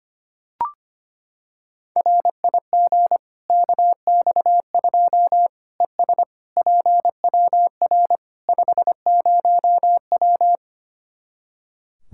Morse code, heard as a keyed tone, about 25 wpm, 700 Hz. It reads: RIG KX2 ES PWR 50W
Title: Practice QSO - Exchange 2 - Info - 25wpm